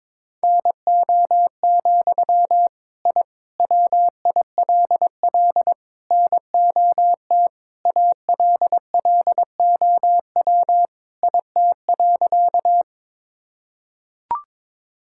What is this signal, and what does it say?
Morse code, heard as a keyed tone, about 21 wpm, 700 Hz. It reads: NO, I WILL NOT ALLOW IT.
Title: Sentences from Top 1000 Words - Review Part 5 - 22wpm